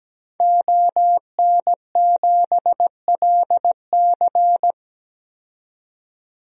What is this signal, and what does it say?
Morse code, heard as a keyed tone, about 17 wpm, 700 Hz. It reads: ON7LC
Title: DXCC-100 Call Signs - 5 Characters - 17wpm